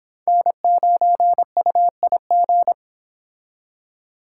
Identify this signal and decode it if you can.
Morse code, heard as a keyed tone, about 26 wpm, 700 Hz. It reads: N9UIG